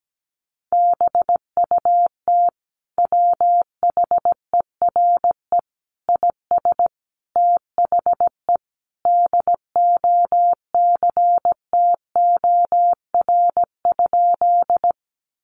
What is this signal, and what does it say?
Morse code, heard as a keyed tone, about 17 wpm, 700 Hz. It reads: BUT WHERE IS THE DOCTOR?
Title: Sentences from Top 1500 Words 17wpm